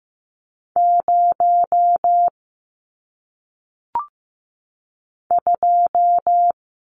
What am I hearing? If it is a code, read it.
Morse code, heard as a keyed tone, about 15 wpm, 700 Hz. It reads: 0 2
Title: Numbers - 1-Digit 15wpm